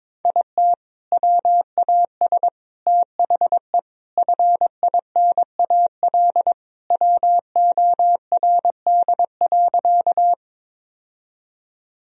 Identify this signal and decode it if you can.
Morse code, heard as a keyed tone, about 22 wpm, 700 Hz. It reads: IT WAS THE FINAL WORD.
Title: Sentences from Top 1000 Words - Review Part 2 - 22wpm